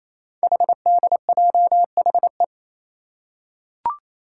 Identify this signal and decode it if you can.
Morse code, heard as a keyed tone, about 28 wpm, 690 Hz. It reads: HDJHE
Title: Code Groups - 5 Letters - 28wpm